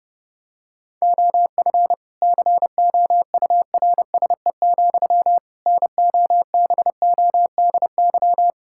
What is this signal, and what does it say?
Morse code, heard as a keyed tone, about 30 wpm, 700 Hz. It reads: OF COURSE, NOBODY
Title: Sentences from Top 1600 Words 30wpm